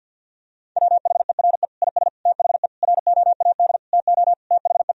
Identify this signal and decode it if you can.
Morse code, heard as a keyed tone, about 50 wpm, 700 Hz. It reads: WHERE IS THE ROAD TO THE
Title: Sentences from Top 1000 Words - Review Part 2 - 50wpm